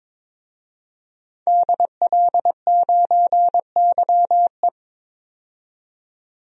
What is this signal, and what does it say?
Morse code, heard as a keyed tone, about 21 wpm, 700 Hz. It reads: DL9YE
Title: DXCC-100 Call Signs - 5 Characters - 22wpm